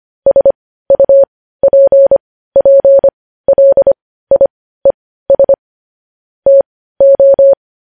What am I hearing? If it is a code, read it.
Morse code, heard as a keyed tone, about 25 wpm, spaced as for 15 wpm, 550 Hz. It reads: SUPPLIES TO
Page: news headlines in morse code at 15 wpm